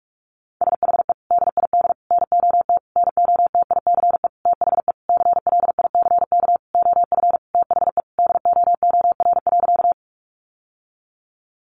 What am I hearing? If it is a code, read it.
Morse code, heard as a keyed tone, about 44 wpm, 700 Hz. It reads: SHE DID NOT NOTICE THE CLICK OF THE DOOR.
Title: Sentences from Top 1100 Words 45wpm